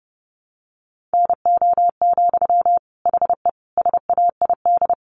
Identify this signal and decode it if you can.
Morse code, heard as a keyed tone, about 30 wpm, 700 Hz. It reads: NO, HE SAID
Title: Sentences from Top 1100 Words 30wpm